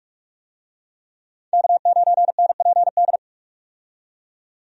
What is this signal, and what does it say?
Morse code, heard as a keyed tone, about 44 wpm, 700 Hz. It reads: K9NPD
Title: USA Call Signs - 5 Characters - 45wpm